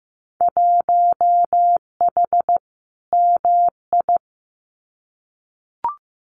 Morse code, heard as a keyed tone, about 15 wpm, 700 Hz. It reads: 1H MI
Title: Winter Field Day Exchanges 15wpm